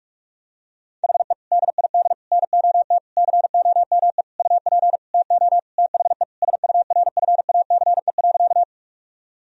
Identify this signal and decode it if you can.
Morse code, heard as a keyed tone, about 47 wpm, 700 Hz. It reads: HE DID NOT COME UP TO THE SURFACE.
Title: Sentences from Top 1000 Words - Review Part 3 - 45wpm